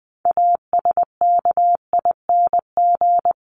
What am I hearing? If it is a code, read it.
Morse code, heard as a keyed tone, about 20 wpm, 700 Hz. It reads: ASKING
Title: ING Suffix - Top 100 Words with Base Words 20wpm